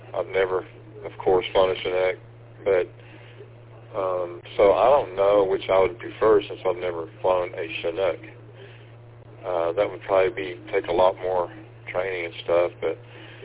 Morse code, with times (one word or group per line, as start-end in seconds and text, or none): none